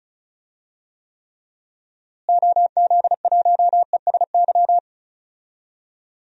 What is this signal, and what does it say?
Morse code, heard as a keyed tone, about 34 wpm, 700 Hz. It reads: OZ1ESY
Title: DXCC-100 Call Signs - 6 Characters - Repeated 2x - 35wpm